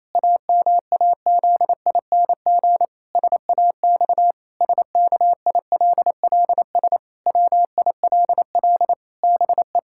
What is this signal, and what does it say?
Morse code, heard as a keyed tone, about 28 wpm, 700 Hz. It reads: AMAZING SAX SKILLS WILL BE
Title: Pangram Challenge v2 - 28wpm